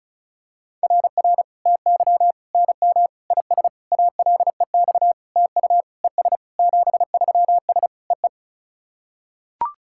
Morse code, heard as a keyed tone, about 34 wpm, 690 Hz. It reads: RR TY NM IS ALEX TU ES 73S EE
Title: SOTA QSO Practice 35wpm